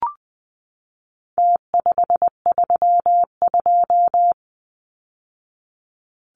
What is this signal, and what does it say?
Morse code, heard as a keyed tone, about 20 wpm, 700 Hz. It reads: T532
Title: Numbers - Contest Style - 4-Digit 20wpm